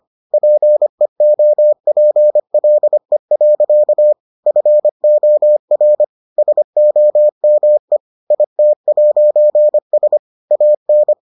Morse code, heard as a keyed tone, about 25 wpm, 600 Hz. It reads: PEOPLE. FOR SOME IT'S AN